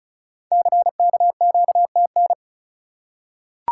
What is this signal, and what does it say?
Morse code, heard as a keyed tone, about 35 wpm, 700 Hz. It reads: CKQTN
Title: Code Groups - 5 Letters - 35wpm